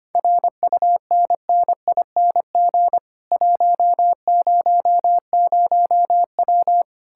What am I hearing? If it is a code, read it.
Morse code, heard as a keyed tone, about 31 wpm, 700 Hz. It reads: RUNNING 100W
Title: Speed-Racing Common QSO Phrases - 25wpm